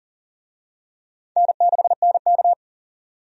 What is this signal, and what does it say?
Morse code, heard as a keyed tone, about 40 wpm, 700 Hz. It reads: N6NK
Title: USA Call Signs - 4 Characters - 40wpm